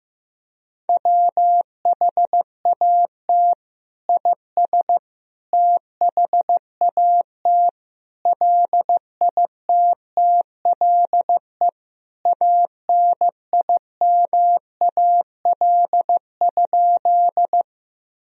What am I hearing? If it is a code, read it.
Morse code, heard as a keyed tone, about 15 wpm, 700 Hz. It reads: WHAT IS THAT LITTLE ANIMAL?
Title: Sentences from Top 1000 Words - Review Part 1 - 15wpm